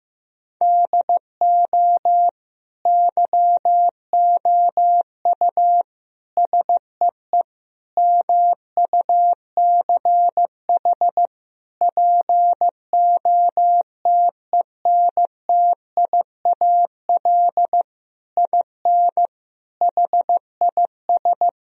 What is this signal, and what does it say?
Morse code, heard as a keyed tone, about 15 wpm, 700 Hz. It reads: DO YOU SEE MUCH POTENTIAL IN HIS
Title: Sentences from Top 1000 Words - Review Part 4 - 15wpm